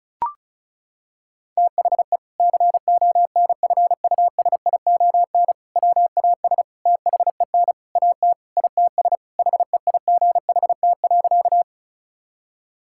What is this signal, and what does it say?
Morse code, heard as a keyed tone, about 36 wpm, 700 Hz. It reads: THE CONFUSION WAS THEN AT ITS HEIGHT.
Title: Sentences from Top 1600 Words 35wpm